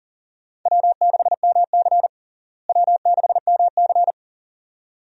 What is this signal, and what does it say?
Morse code, heard as a keyed tone, about 40 wpm, 700 Hz. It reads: W6MC W6MC
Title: USA Call Signs - 4 Characters - Repeated 2x - 40wpm